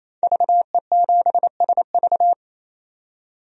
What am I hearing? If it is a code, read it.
Morse code, heard as a keyed tone, about 27 wpm, 700 Hz. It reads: VE7SV
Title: DXCC-100 Call Signs - 5 Characters - 28wpm